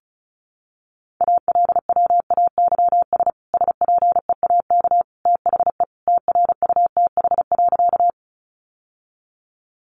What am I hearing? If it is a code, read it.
Morse code, heard as a keyed tone, about 34 wpm, 700 Hz. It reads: ALWAYS SPEAK THE TRUTH.
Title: Sentences from Top 1100 Words 35wpm